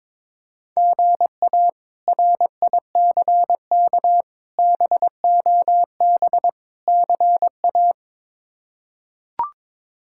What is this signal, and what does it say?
Morse code, heard as a keyed tone, about 22 wpm, 700 Hz. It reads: GA RICK BOB CA
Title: K1USN SST QSO Practice - 22wpm